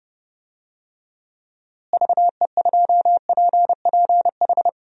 Morse code, heard as a keyed tone, about 30 wpm, 700 Hz. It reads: VE2PPH
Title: DXCC-100 Call Signs - 6 Characters - Repeated 3x - 30wpm